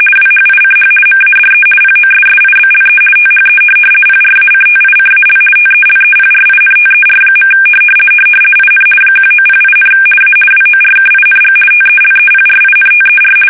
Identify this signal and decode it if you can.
RTTY, about 50 baud. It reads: WHICH MADE POSSIBLE THE PRACTICAL TELEPRINTER. IN 1908, A WORKING TELEPRINTER
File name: fsk850-50B / teletype-corp